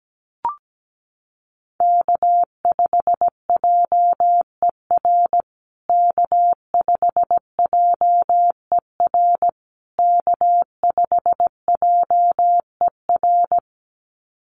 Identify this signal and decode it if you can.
Morse code, heard as a keyed tone, about 17 wpm, 700 Hz. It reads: K5JER K5JER K5JER
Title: USA Call Signs Repeated 3x 17wpm